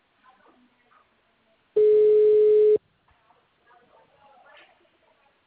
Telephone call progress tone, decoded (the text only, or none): ringback tone